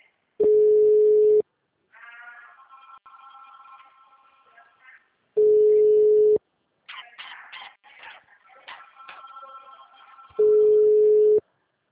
Ringback tone (Europe, 425 Hz).